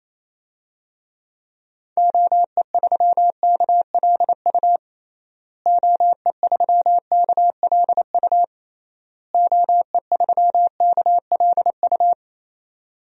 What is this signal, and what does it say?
Morse code, heard as a keyed tone, about 28 wpm, 700 Hz. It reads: OE3KLU OE3KLU OE3KLU
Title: DXCC-100 Call Signs - 6 Characters - Repeated 3x - 28wpm